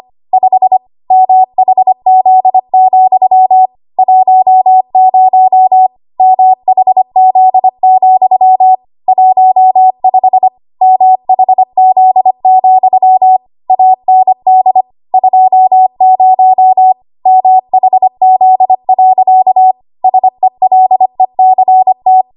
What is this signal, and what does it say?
Morse code, heard as a keyed tone, about 25 wpm, 750 Hz. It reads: 5 MHZ, 10 MHZ, 15 MHZ, AND 20 MHZ. SELECT
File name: Archive / 25 WPM / 25WPM